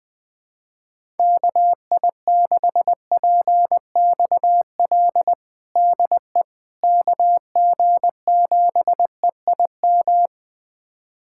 Morse code, heard as a keyed tone, about 21 wpm, 700 Hz. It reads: KI6PXL DE KG7EIM
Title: USA Callsign DE Callsign 20wpm